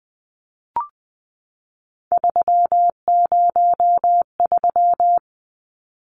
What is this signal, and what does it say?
Morse code, heard as a keyed tone, about 20 wpm, 700 Hz. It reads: 303
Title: Numbers - 3-Digit 20wpm